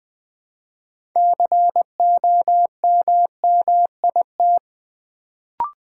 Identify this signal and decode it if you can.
Morse code, heard as a keyed tone, about 20 wpm, 700 Hz. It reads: COMMIT
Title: Build A Word - 3 Letters At A Time - 6-Letter Words - 20wpm